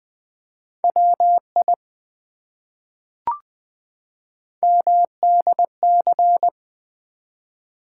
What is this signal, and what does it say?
Morse code, heard as a keyed tone, about 20 wpm, 700 Hz. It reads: WI MDC
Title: ARRL Sections v2 - 20wpm